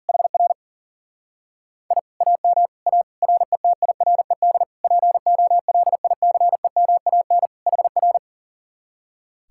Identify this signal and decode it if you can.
Morse code, heard as a keyed tone, about 35 wpm, 700 Hz. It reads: HR I AM A RETIRED POLICEMAN HR
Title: Speed-Racing Common QSO Phrases - 40wpm